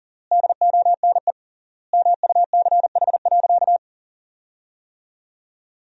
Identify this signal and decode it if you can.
Morse code, heard as a keyed tone, about 40 wpm, 700 Hz. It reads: DONE MUCH.
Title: 3x Word-Spacing - Sentences from Top 500 Words - 40wpm